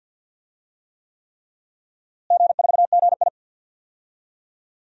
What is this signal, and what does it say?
Morse code, heard as a keyed tone, about 48 wpm, 700 Hz. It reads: G4GI
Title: DXCC-100 Call Signs - 4 Characters - Repeated 3x - 50wpm